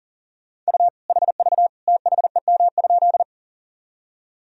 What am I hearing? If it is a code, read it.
Morse code, heard as a keyed tone, about 40 wpm, 700 Hz. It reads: U HV THEM?
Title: Sentences from Top 100 Words and Abbreviations 40wpm